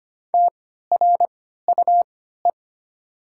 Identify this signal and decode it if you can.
Morse code, heard as a keyed tone, about 25 wpm, spaced as for 14 wpm, 700 Hz. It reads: TRUE
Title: 3x Character Spacing - Top 500 Words - 25wpm